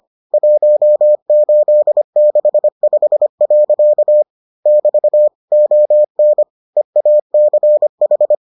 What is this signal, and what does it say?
Morse code, heard as a keyed tone, about 25 wpm, 600 Hz. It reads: 1865. <BT> ON EACH